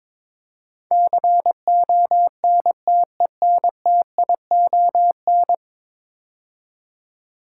Morse code, heard as a keyed tone, about 22 wpm, 700 Hz. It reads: CONTENTION
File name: practice / 22wpm / 10 Letter Words - First 3 Letters Given - 22wpm